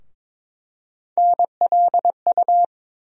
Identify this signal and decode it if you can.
Morse code, heard as a keyed tone, about 21 wpm, 700 Hz. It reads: NLU
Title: USA Call Sign Postfixes - 3 Letters 22wpm